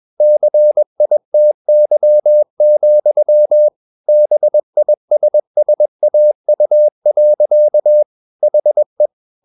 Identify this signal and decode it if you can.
Morse code, heard as a keyed tone, about 21 wpm, 600 Hz. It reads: CITY, BISSAU. HE
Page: CW med Gnister